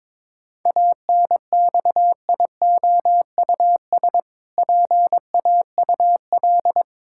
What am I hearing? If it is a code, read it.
Morse code, heard as a keyed tone, about 22 wpm, 700 Hz. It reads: ANXIOUS PAUL
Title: Pangram Challenge v2 - 22wpm